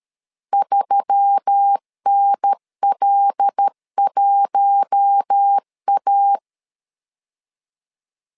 Morse code, heard as a keyed tone, about 13 wpm, 800 Hz. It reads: 3NL1A
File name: LEZIONI / 5 LBVC / 5 L 12wpm 1913